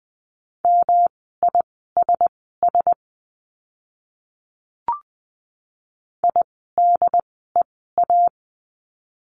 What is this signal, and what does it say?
Morse code, heard as a keyed tone, about 20 wpm, spaced as for 14 wpm, 700 Hz. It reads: MISS IDEA
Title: 108 Lesson 18f - Letter m - Words 20wpm